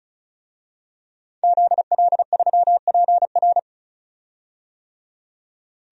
Morse code, heard as a keyed tone, about 36 wpm, 700 Hz. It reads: ZL3PR
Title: DXCC-100 Call Signs - 5 Characters - 35wpm